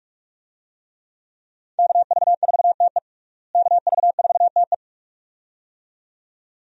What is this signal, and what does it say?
Morse code, heard as a keyed tone, about 44 wpm, 700 Hz. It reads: KV4TE KV4TE
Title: USA Call Signs - 5 Characters - Repeated 2x - 45wpm